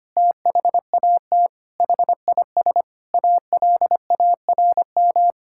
Morse code, heard as a keyed tone, about 25 wpm, 700 Hz. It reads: THAT HIS ALARM
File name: practice / 25wpm / Sentences with 4-Word Phrases - 25wpm